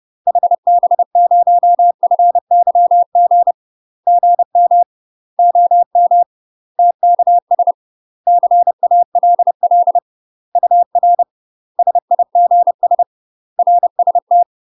Morse code, heard as a keyed tone, about 30 wpm, 700 Hz. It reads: HB0FYG GM OM TKS CALL UR SIGS RST
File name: UK MASTER QSOs TRAINING / 30WPM / UK MASTER QSOs 0020